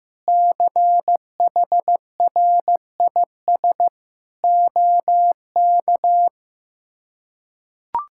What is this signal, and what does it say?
Morse code, heard as a keyed tone, about 15 wpm, 700 Hz. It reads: CHRIS OK
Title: K1USN SST QSO Practice - 15wpm